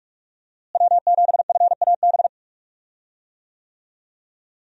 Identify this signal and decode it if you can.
Morse code, heard as a keyed tone, about 45 wpm, 700 Hz. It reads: W7FAB
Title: USA Call Signs - 5 Characters - Repeated 3x - 45wpm